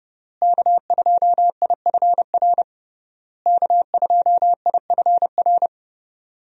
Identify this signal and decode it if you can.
Morse code, heard as a keyed tone, about 31 wpm, 700 Hz. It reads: K2IFR K2IFR
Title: USA Call Signs - 5 Characters - Repeated 3x - 30wpm